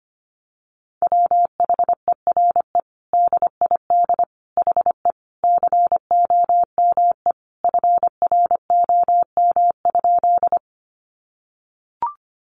Morse code, heard as a keyed tone, about 25 wpm, 700 Hz. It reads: WHERE DID HE COME FROM?
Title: Sentences from Top 200 Words 25wpm